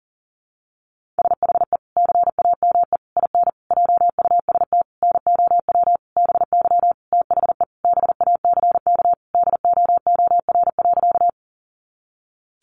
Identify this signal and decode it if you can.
Morse code, heard as a keyed tone, about 40 wpm, 700 Hz. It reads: SHE CAME IN JUST NOW BY THE BACK DOOR.
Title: Sentences from Top 300 Words 40wpm